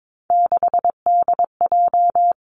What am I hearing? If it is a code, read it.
Morse code, heard as a keyed tone, about 22 wpm, 700 Hz. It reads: 6DJ
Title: Three Challenging Letter-Number Combinations 22wpm